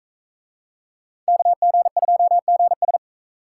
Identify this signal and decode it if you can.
Morse code, heard as a keyed tone, about 43 wpm, 700 Hz. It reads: KG2GS